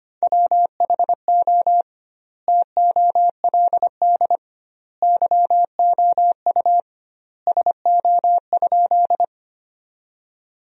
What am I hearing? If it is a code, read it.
Morse code, heard as a keyed tone, about 25 wpm, 700 Hz. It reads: WHO TOLD YOU SO?
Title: 2x Word-Spacing - Sentences from Top 400 Words - 25wpm